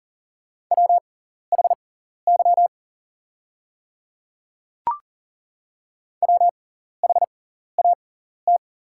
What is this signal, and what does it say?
Morse code, heard as a keyed tone, about 39 wpm, spaced as for 14 wpm, 700 Hz. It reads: WHY WHAT
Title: 6x Character Spacing - Top 200 Words - 40wpm